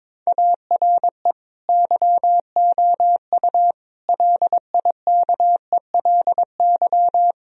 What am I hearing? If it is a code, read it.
Morse code, heard as a keyed tone, about 22 wpm, 700 Hz. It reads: ARE YOU LIKELY
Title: Sentences from Top 1000 Words - Review Part 4 - 22wpm